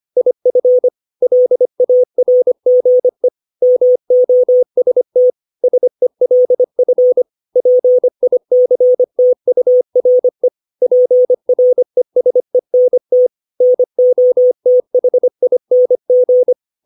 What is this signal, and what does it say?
Morse code, heard as a keyed tone, about 25 wpm, 500 Hz. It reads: IF LARGE MOST SELF PICTURE PRESENT NOTHING
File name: 632 words / 632 Words 25WPM